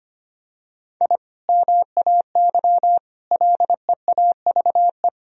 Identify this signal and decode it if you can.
Morse code, heard as a keyed tone, about 25 wpm, 700 Hz. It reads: I MAY LEAVE
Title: Sentences from Top 1000 Words - Review Part 4 - 25wpm